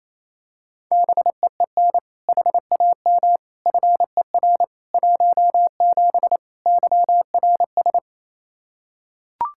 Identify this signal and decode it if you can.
Morse code, heard as a keyed tone, about 28 wpm, 700 Hz. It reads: BEEN HAM FER 17 YRS
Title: K1USN SST QSO Practice 28wpm